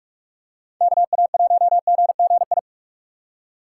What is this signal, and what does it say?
Morse code, heard as a keyed tone, about 46 wpm, 700 Hz. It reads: KA1GGI